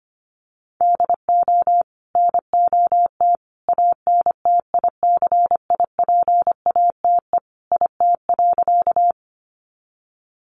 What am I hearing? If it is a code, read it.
Morse code, heard as a keyed tone, about 25 wpm, 700 Hz. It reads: DO NOT ANTICIPATE IT.
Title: Sentences from Top 1500 Words 25wpm